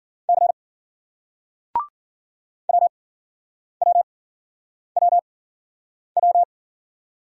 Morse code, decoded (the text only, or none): C WWWW